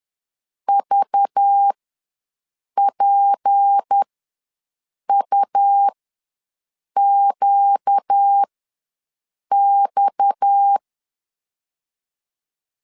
Morse code, heard as a keyed tone, about 10 wpm, spaced as for 6 wpm, 800 Hz. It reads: VPUQX
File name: LEZIONI / 6 PKX0 / 6 PKX0 ws 10wpm 4220